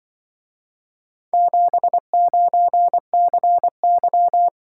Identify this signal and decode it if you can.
Morse code, heard as a keyed tone, about 25 wpm, 700 Hz. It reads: 79CY